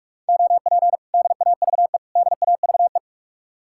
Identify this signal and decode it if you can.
Morse code, heard as a keyed tone, about 56 wpm, 700 Hz. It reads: OP DAVE DAVE
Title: Speed-Racing Common QSO Phrases Encore - 45wpm